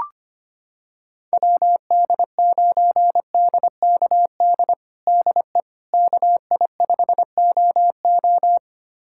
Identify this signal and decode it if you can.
Morse code, heard as a keyed tone, about 25 wpm, 700 Hz. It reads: WD9DKD DE KI5OO
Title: USA Callsign DE Callsign 25wpm